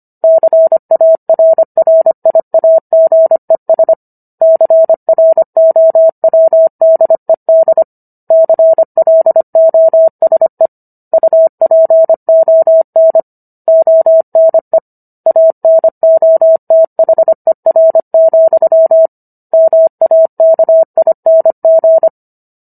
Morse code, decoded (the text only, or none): CARRIAGES CROWDED CLOSE UPON ONE ANOTHER, MAKING